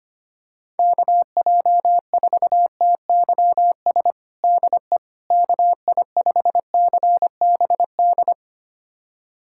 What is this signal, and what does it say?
Morse code, heard as a keyed tone, about 25 wpm, 700 Hz. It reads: KJ4TYS DE KI5CBD
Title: USA Callsign DE Callsign - 6 Characters - 25wpm